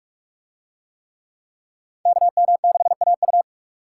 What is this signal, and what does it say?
Morse code, heard as a keyed tone, about 44 wpm, 700 Hz. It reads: KM6AU